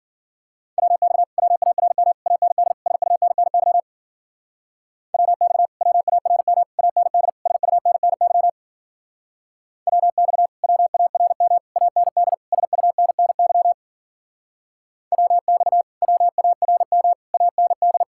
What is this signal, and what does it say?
Morse code, heard as a keyed tone, about 51 wpm, 700 Hz. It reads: WX WARM AND SUNNY WX WARM AND SUNNY WX WARM AND SUNNY WX WARM AND